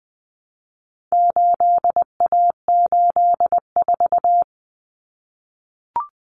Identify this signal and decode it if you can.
Morse code, heard as a keyed tone, about 20 wpm, 700 Hz. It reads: 8A84
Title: Common Cut Numbers - 4-Digit 20wpm